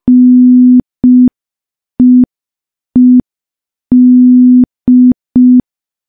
Morse code, heard as a keyed tone, about 5 wpm, 250 Hz. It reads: NEED
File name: morse / morse-training-05WPM